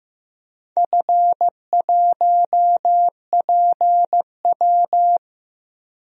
Morse code, heard as a keyed tone, about 16 wpm, 700 Hz. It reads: F1PW
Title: DXCC-100 Call Signs - 4 Characters - Repeated 2x - 15wpm